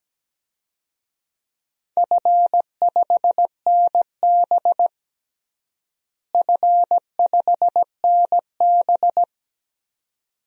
Morse code, heard as a keyed tone, about 17 wpm, 700 Hz. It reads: F5NB F5NB
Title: DXCC-100 Call Signs - 4 Characters - Repeated 2x - 17wpm